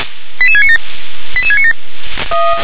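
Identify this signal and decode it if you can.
Five-tone selective call (ZVEI), heard as quick runs of five tones, tones of about 70 ms. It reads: 88697 88697
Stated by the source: ZVEI five-tone selective call received on an SDR.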